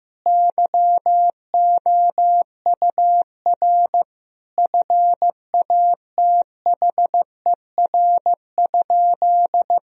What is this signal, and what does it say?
Morse code, heard as a keyed tone, about 15 wpm, 700 Hz. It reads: YOUR FATHER?
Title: Sentences from Top 1000 Words - Review Part 3 - 15wpm